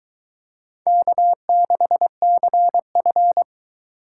Morse code, heard as a keyed tone, about 22 wpm, 700 Hz. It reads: K6CF